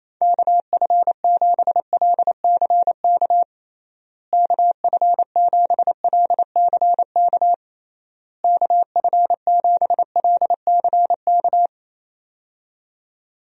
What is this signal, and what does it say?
Morse code, heard as a keyed tone, about 28 wpm, 700 Hz. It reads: KF7LCK KF7LCK KF7LCK
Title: USA Call Signs - 6 Characters - Repeated 3x - 28wpm